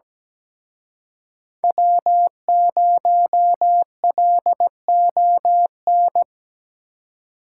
Morse code, heard as a keyed tone, about 20 wpm, 700 Hz. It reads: W0LON
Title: Speed Racing - USA 1x3 Call Signs 17wpm